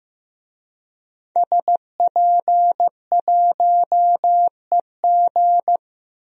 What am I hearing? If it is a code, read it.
Morse code, heard as a keyed tone, about 15 wpm, 700 Hz. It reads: SP1EG
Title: DXCC-100 Call Signs - 5 Characters - 15wpm